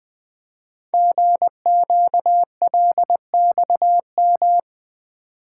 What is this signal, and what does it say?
Morse code, heard as a keyed tone, about 20 wpm, 700 Hz. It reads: GQLXM
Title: Code Groups - 5 Letters - 20wpm